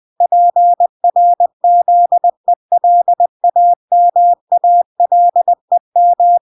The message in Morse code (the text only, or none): PRZELAMALEM